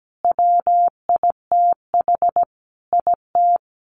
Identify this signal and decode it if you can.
Morse code, heard as a keyed tone, about 17 wpm, 700 Hz. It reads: WITH IT
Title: Sets of 5 Words Encore 17wpm